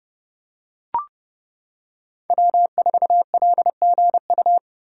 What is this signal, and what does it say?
Morse code, heard as a keyed tone, about 28 wpm, 700 Hz. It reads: W4LGU
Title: Code Groups - 5 Letter:Numbers - 30wpm